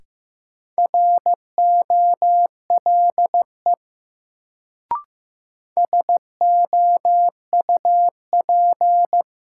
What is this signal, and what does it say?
Morse code, heard as a keyed tone, about 15 wpm, 700 Hz. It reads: ROLE SOUP
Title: Sentences from Top 1000 Words - Review Part 6 - New Word Drill - 15wpm